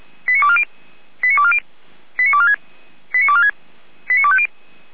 Five-tone selective call (ZVEI), heard as quick runs of five tones, tones of about 80 ms. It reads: 89240 89240 89246 89246 89250